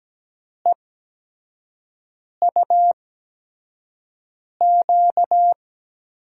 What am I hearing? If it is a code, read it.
Morse code, heard as a keyed tone, about 17 wpm, spaced as for 5 wpm, 700 Hz. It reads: EUQ
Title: USA Call Signs - 8x Character Spacing - 17wpm